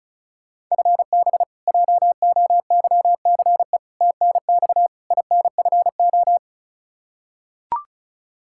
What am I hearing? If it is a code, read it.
Morse code, heard as a keyed tone, about 36 wpm, 700 Hz. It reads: FB JOYCE TNX INFO
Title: K1USN SST QSO Practice 35wpm